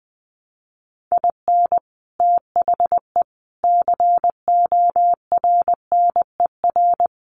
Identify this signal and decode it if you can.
Morse code, heard as a keyed tone, about 20 wpm, 700 Hz. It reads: IN THE CORNER
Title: Sets of 5 Words Encore 20wpm